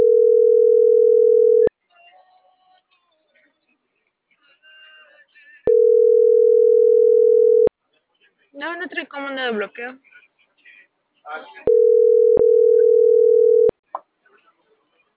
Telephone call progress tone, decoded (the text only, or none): ringback tone